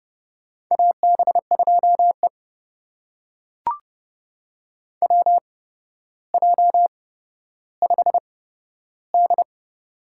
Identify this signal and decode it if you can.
Morse code, heard as a keyed tone, about 31 wpm, spaced as for 13 wpm, 700 Hz. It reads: AB2E WJ5D
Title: USA Call Signs - 4 Characters - 8x Character Spacing - 30wpm